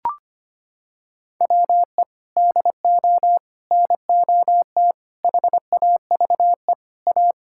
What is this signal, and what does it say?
Morse code, heard as a keyed tone, about 25 wpm, 700 Hz. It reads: WE DO NOT HAVE A
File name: practice / 25wpm / Sentences with 3-Word Phrases - 25wpm